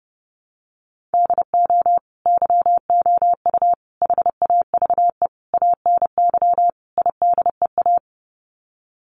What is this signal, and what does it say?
Morse code, heard as a keyed tone, about 28 wpm, 700 Hz. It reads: DO YOU HAVE ANY IDEA
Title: Sets of 5 Words Encore 30wpm